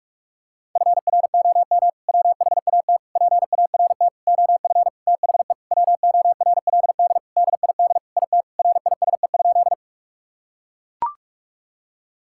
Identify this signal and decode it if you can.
Morse code, heard as a keyed tone, about 42 wpm, 690 Hz. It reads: FROM WHAT PART OF THE WORLD DID IT RISE?
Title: Sentences from Top 1000 Words - Review Part 6 - 45wpm